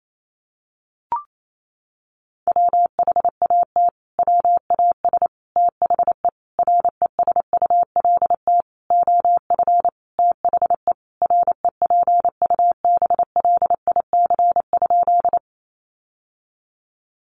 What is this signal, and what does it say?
Morse code, heard as a keyed tone, about 28 wpm, 700 Hz. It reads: WHAT WAS THE RESULT OF THE REPUBLIC?
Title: Sentences from Top 1500 Words 28wpm